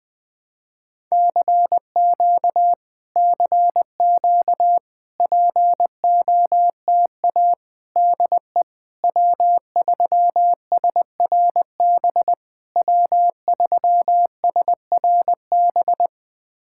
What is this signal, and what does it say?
Morse code, heard as a keyed tone, about 20 wpm, 700 Hz. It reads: CQ CQ POTA DE W3SRB W3SRB
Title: POTA QSO Practice - 20wpm